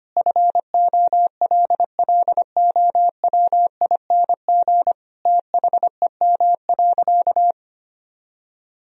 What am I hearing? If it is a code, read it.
Morse code, heard as a keyed tone, about 25 wpm, 700 Hz. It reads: FOLLOWING THEM.
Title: Sentences from Top 1000 Words - Review Part 4 - 25wpm